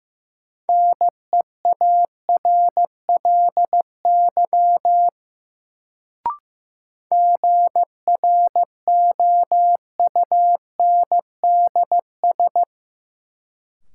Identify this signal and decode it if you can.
Morse code, heard as a keyed tone, about 15 wpm, 700 Hz. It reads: NEARLY GROUNDS
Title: Sentences from Top 1300 Words - New Word Drill Alt - 15wpm